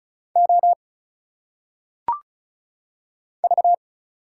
Morse code, heard as a keyed tone, about 36 wpm, 700 Hz. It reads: O V
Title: Single Letters - Phonetic Pronunciation 35wpm